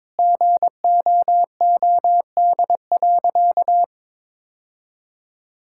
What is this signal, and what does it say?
Morse code, heard as a keyed tone, about 22 wpm, 700 Hz. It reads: GOOD.
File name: practice / 22wpm / Sentences with 3-Word Phrases - 22wpm